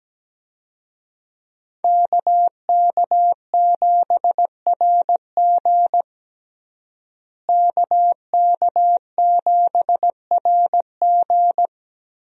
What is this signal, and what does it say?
Morse code, heard as a keyed tone, about 17 wpm, 700 Hz. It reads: KK7RG KK7RG
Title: USA Call Signs - 5 Characters - Repeated 3x - 17wpm